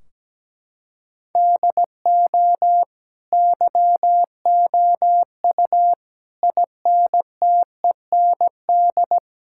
Morse code, heard as a keyed tone, about 17 wpm, 700 Hz. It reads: DO YOU INTEND
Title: Sentences from Top 1600 Words 17wpm